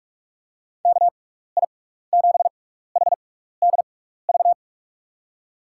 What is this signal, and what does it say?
Morse code, heard as a keyed tone, about 44 wpm, spaced as for 15 wpm, 700 Hz. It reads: KI7HDV
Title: USA Call Signs - 6x Character Spacing - 45wpm